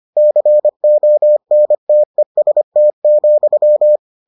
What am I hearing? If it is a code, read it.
Morse code, heard as a keyed tone, about 25 wpm, 600 Hz. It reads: CONTEST,